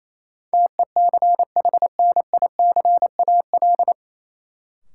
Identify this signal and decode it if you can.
Morse code, heard as a keyed tone, about 28 wpm, 700 Hz. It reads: TECHNICAL
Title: Sentences from Top 1100 Words - New Word Drill - 28wpm